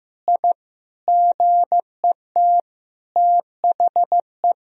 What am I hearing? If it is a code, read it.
Morse code, heard as a keyed tone, about 15 wpm, 700 Hz. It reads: I GET THE
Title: Sentences from Top 1000 Words - Review Part 3 - 15wpm